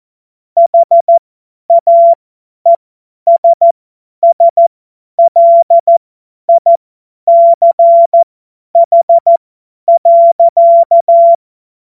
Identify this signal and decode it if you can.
Morse code, heard as a keyed tone, about 14 wpm, spaced as for 10 wpm, 680 Hz. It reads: HAESSLICH.
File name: Morsekurs / Witze / Witz 07. Dezember 2021, 50 F70